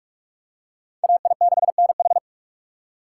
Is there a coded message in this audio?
Yes: Morse code from 1.0 to 2.2 s, heard as a keyed tone, about 41 wpm, 700 Hz. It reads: AI6NH